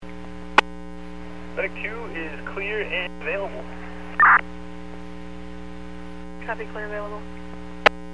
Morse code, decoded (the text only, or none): none